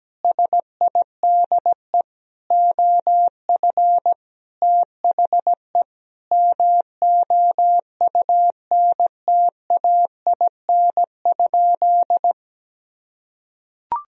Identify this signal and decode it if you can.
Morse code, heard as a keyed tone, about 17 wpm, 700 Hz. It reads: SIDE OF THE MOUNTAIN?
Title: Sentences from Top 1000 Words - Review Part 1 - 17wpm